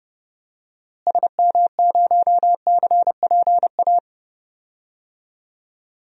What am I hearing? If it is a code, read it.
Morse code, heard as a keyed tone, about 32 wpm, 700 Hz. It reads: SM0CPA
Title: DXCC-100 Call Signs - 6 Characters - Repeated 3x - 30wpm